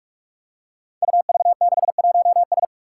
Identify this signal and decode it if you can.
Morse code, heard as a keyed tone, about 44 wpm, 700 Hz. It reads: UV61S
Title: Five Challenging Letter-Number Combinations 45wpm